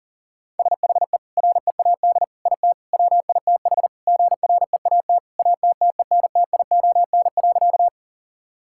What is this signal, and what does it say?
Morse code, heard as a keyed tone, about 40 wpm, 700 Hz. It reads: SHE READ IT WITH GREAT ATTENTION.
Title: Sentences from Top 1000 Words - Review Part 6 - 40wpm